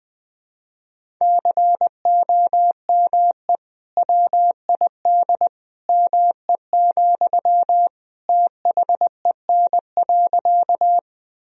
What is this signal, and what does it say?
Morse code, heard as a keyed tone, about 20 wpm, 700 Hz. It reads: COME WID ME, THEN.
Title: Sentences from Top 200 Words and Abbreviations 20wpm